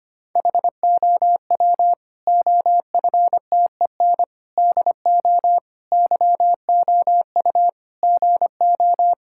Morse code, heard as a keyed tone, about 25 wpm, 700 Hz. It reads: HOW OFTEN DO YOU GO
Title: Sentences from Top 1000 Words - Review Part 5 - 25wpm